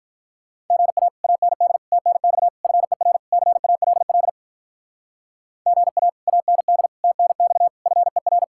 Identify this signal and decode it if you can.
Morse code, heard as a keyed tone, about 44 wpm, 700 Hz. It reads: GA AND TNX FER CALL GA AND TNX FER
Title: Speed-Racing Common QSO Phrases Encore - 35wpm